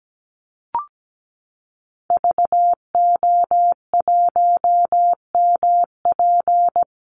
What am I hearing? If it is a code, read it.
Morse code, heard as a keyed tone, about 17 wpm, 700 Hz. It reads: VO1MP
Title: Call Signs Repeated 3x - 17wpm